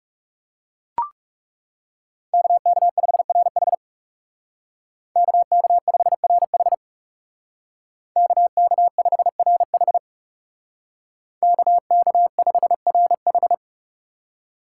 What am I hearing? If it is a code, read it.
Morse code, heard as a keyed tone, about 39 wpm, 700 Hz. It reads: KK5RH KK5RH KK5RH KK5RH